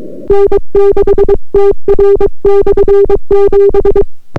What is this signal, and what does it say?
Morse code, heard as a keyed tone, about 22 wpm, 400 Hz. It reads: N6TR/7